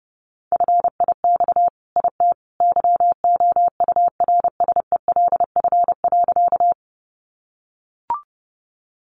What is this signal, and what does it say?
Morse code, heard as a keyed tone, about 30 wpm, 700 Hz. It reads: FIX IT YOURSELF.
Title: Sentences from Top 1400 Words 30wpm